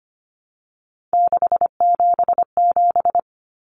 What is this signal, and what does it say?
Morse code, heard as a keyed tone, about 25 wpm, 700 Hz. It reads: 677
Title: Numbers - 3-Digit 25wpm